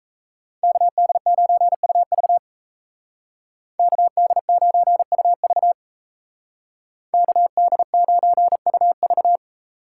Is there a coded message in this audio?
Yes: Morse code from 0.6 to 9.4 s, heard as a keyed tone, about 36 wpm, 700 Hz. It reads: KD9UV KD9UV KD9UV